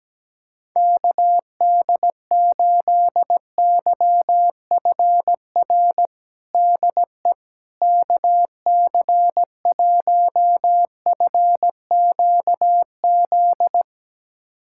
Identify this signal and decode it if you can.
Morse code, heard as a keyed tone, about 17 wpm, 700 Hz. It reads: KD8YFR DE KC1FQZ
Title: USA Callsign DE Callsign 17wpm